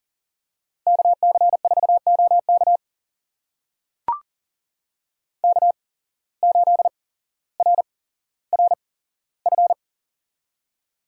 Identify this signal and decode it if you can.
Morse code, heard as a keyed tone, about 38 wpm, spaced as for 17 wpm, 700 Hz. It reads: KC4OK K8RRF
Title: USA Call Signs - 5 Characters - 8x Character Spacing - 40wpm